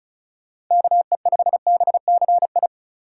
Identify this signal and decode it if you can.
Morse code, heard as a keyed tone, about 34 wpm, 700 Hz. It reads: KE5BCI
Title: USA Call Signs - 6 Characters - 2x Character Spacing - 35wpm